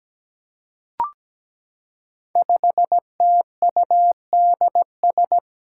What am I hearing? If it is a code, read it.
Morse code, heard as a keyed tone, about 17 wpm, 700 Hz. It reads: 5TUDS
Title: Five Challenging Letter-Number Combinations 17wpm